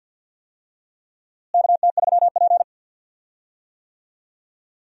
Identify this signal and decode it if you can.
Morse code, heard as a keyed tone, about 51 wpm, 700 Hz. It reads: KT3P
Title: USA Call Signs - 4 Characters - Repeated 2x - 50wpm